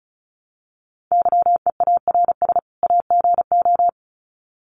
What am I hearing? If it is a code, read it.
Morse code, heard as a keyed tone, about 34 wpm, 700 Hz. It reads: YEARS AGO
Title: Sets of 2 Words 35wpm